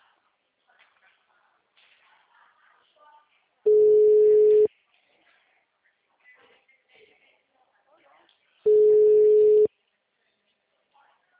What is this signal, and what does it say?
Ringback tone (Europe, 425 Hz), with speech.